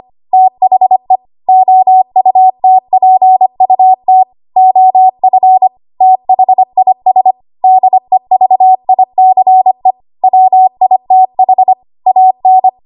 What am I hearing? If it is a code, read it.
Morse code, heard as a keyed tone, about 25 wpm, 750 Hz. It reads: THE OUTPUT OF THIS DEVICE WITH AN
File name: Archive / 25 WPM / 25WPM